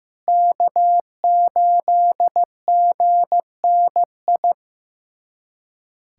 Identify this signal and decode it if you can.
Morse code, heard as a keyed tone, about 15 wpm, 700 Hz. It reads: K8GNI
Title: USA Callsign DE Callsign - 5 Characters - 15wpm